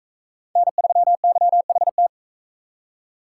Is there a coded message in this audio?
Yes: Morse code from 0.6 to 2.1 s, heard as a keyed tone, about 44 wpm, 700 Hz. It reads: N3YHT